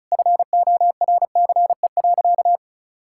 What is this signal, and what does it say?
Morse code, heard as a keyed tone, about 36 wpm, 700 Hz. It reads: FORCE.
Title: Sentences from Top 1000 Words - Review Part 6 - 35wpm